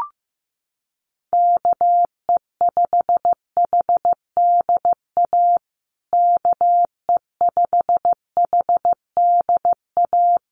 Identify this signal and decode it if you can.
Morse code, heard as a keyed tone, about 15 wpm, 700 Hz. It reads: KE5HDA KE5HDA
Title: USA Call Signs Repeated 2x 15wpm